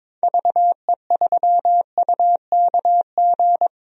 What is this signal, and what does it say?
Morse code, heard as a keyed tone, about 22 wpm, 700 Hz. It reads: VE3UKG
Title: DXCC-100 Call Signs - 6 Characters - Repeated 3x - 22wpm